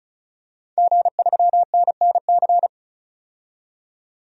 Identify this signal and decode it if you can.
Morse code, heard as a keyed tone, about 34 wpm, 700 Hz. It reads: G3NNC
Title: DXCC-100 Call Signs - 5 Characters - 35wpm